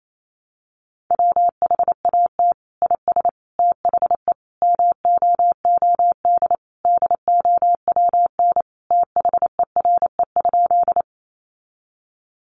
Morse code, heard as a keyed tone, about 28 wpm, 700 Hz. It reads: WHAT IS THE MOOD DOWN THERE?
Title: Sentences from Top 1100 Words 28wpm